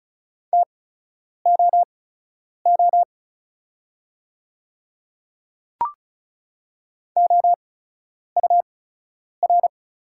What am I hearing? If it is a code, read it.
Morse code, heard as a keyed tone, about 36 wpm, spaced as for 10 wpm, 690 Hz. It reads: TOO OUR
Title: 8x Character Spacing - Top 300 Words - 35wpm